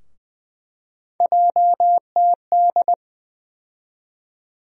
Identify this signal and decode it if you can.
Morse code, heard as a keyed tone, about 20 wpm, 700 Hz. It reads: JTD
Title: USA Call Sign Postfixes - 3 Letters 20wpm